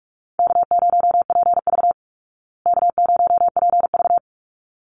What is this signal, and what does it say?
Morse code, heard as a keyed tone, about 44 wpm, 700 Hz. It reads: K0PV K0PV
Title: Call Signs Repeated 3x - 45wpm